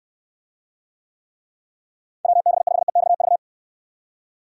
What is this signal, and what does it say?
Morse code, heard as a keyed tone, about 69 wpm, 700 Hz. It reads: WB5EXV